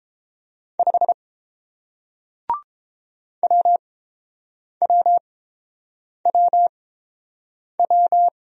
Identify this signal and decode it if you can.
Morse code, heard as a keyed tone, about 29 wpm, spaced as for 7 wpm, 700 Hz. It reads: 5 WWWW